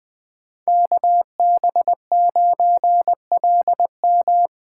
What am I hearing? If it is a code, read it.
Morse code, heard as a keyed tone, about 20 wpm, 700 Hz. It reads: KB9LM
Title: USA Call Signs - 5 Characters - 3x Character Spacing - 20wpm